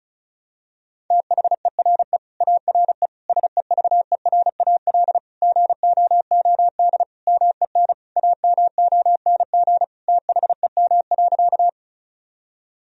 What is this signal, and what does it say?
Morse code, heard as a keyed tone, about 36 wpm, 700 Hz. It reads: THERE ARE SEVERAL GOOD MEN AMONG THEM.
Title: Sentences from Top 1000 Words - Review Part 2 - 35wpm